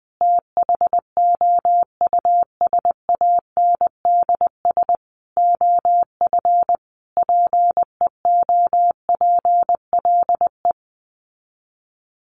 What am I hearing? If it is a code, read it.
Morse code, heard as a keyed tone, about 20 wpm, 700 Hz. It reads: THOUSANDS OF PEOPLE
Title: Sets of 5 Words Encore2 20wpm